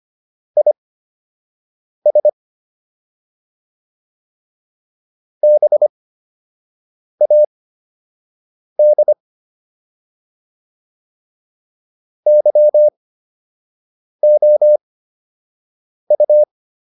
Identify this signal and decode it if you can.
Morse code, heard as a keyed tone, about 25 wpm, spaced as for 6 wpm, 600 Hz. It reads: IS BAD YOU